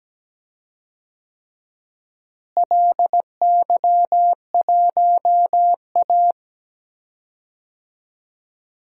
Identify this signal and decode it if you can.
Morse code, heard as a keyed tone, about 17 wpm, 700 Hz. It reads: LY1A
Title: DXCC-100 Call Signs - 4 Characters - Repeated 3x - 17wpm